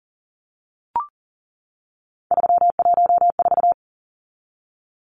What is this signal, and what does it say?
Morse code, heard as a keyed tone, about 37 wpm, 700 Hz. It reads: 314
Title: Numbers - 3-Digit 40wpm